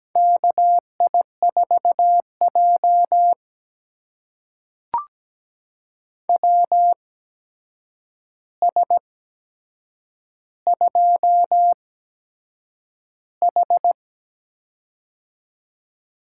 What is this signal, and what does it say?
Morse code, heard as a keyed tone, about 17 wpm, spaced as for 7 wpm, 700 Hz. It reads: KI4J WS2H
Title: USA Call Signs - 4 Characters - 8x Character Spacing - 17wpm